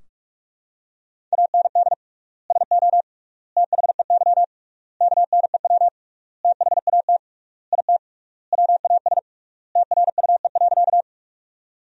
Morse code, heard as a keyed tone, about 46 wpm, 700 Hz. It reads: AND SO THEY KNEW THAT IT WAS TRUE.
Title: 3x Word-Spacing - Sentences from Top 400 Words - 45wpm